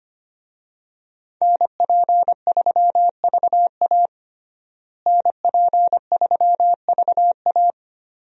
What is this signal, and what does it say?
Morse code, heard as a keyed tone, about 25 wpm, 700 Hz. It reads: NP3VA NP3VA
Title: USA Call Signs - 5 Characters - Repeated 2x - 25wpm